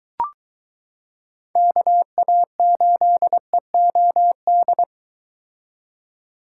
Morse code, heard as a keyed tone, about 22 wpm, 700 Hz. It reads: KA8EOD